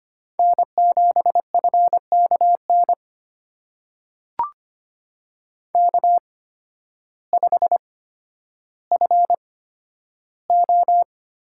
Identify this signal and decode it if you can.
Morse code, heard as a keyed tone, about 25 wpm, spaced as for 12 wpm, 700 Hz. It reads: N7FKN K5FO
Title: USA Call Signs - 5 Characters - 8x Character Spacing - 25wpm